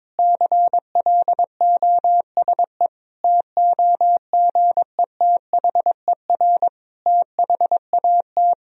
Morse code, heard as a keyed tone, about 22 wpm, 700 Hz. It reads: CLOSE TOGETHER THAT
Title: Sentences from Top 1000 Words - Review Part 1 - 22wpm